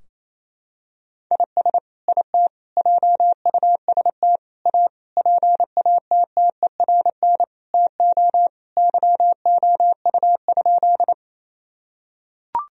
Morse code, heard as a keyed tone, about 27 wpm, 700 Hz. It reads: IS IT JUST A PATTERN TO YOU?
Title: Sentences from Top 1000 Words - Review Part 2 - 28wpm